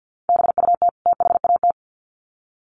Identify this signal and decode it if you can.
Morse code, heard as a keyed tone, about 50 wpm, 700 Hz. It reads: BUT THAT
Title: Sets of 2 Words 50wpm